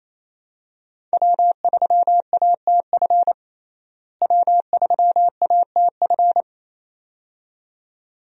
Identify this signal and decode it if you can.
Morse code, heard as a keyed tone, about 28 wpm, 700 Hz. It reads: W3ATF W3ATF
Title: USA Call Signs - 5 Characters - Repeated 2x - 28wpm